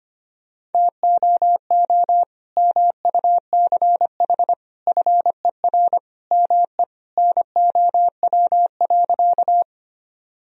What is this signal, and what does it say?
Morse code, heard as a keyed tone, about 25 wpm, 700 Hz. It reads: TOO MUCH FER ME NOW.
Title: Sentences from Top 200 Words and Abbreviations 25wpm